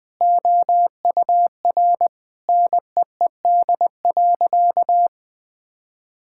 Morse code, heard as a keyed tone, about 19 wpm, 700 Hz. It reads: OUR NEED.
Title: Sentences from Top 1000 Words - Review Part 6 - 20wpm